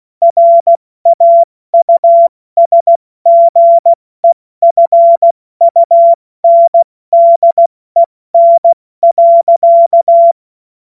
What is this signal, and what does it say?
Morse code, heard as a keyed tone, about 16 wpm, spaced as for 14 wpm, 680 Hz. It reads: RAUSGEFUNDEN.
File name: Morsekurs / Witze / Witz 21. Dezember 2021, 70 F80